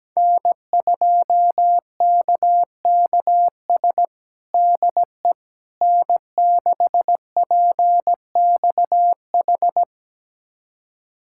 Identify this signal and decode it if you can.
Morse code, heard as a keyed tone, about 17 wpm, 700 Hz. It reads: N2KKS DE N6PXH
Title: USA Callsign DE Callsign - 5 Characters - 17wpm